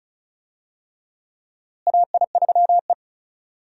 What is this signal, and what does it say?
Morse code, heard as a keyed tone, about 36 wpm, 700 Hz. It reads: AI3E
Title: USA Call Signs - 4 Characters - Repeated 3x - 35wpm